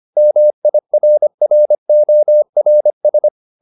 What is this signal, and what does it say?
Morse code, heard as a keyed tone, about 25 wpm, 600 Hz. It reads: MIRRORS